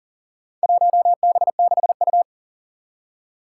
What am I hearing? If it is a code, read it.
Morse code, heard as a keyed tone, about 38 wpm, 700 Hz. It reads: 1B6U
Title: Four Challenging Letter-Number Combinations 40wpm